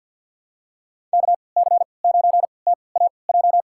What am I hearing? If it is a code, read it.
Morse code, heard as a keyed tone, about 53 wpm, spaced as for 29 wpm, 700 Hz. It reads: KC9TAJ
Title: USA Call Signs - 6 Characters - 3x Character Spacing - 50wpm